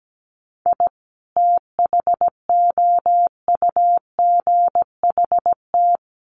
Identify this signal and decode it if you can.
Morse code, heard as a keyed tone, about 17 wpm, 700 Hz. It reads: I THOUGHT
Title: Sets of 2 Words 17wpm